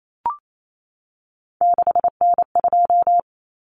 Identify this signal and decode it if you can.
Morse code, heard as a keyed tone, about 28 wpm, 700 Hz. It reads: 6N2
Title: Common Cut Numbers - 3-Digit 28wpm